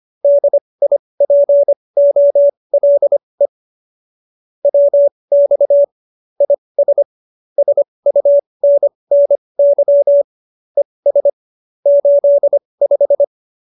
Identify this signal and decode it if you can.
Morse code, heard as a keyed tone, about 25 wpm, spaced as for 20 wpm, 570 Hz. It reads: DIPOLE WX IS SUNNY ES 85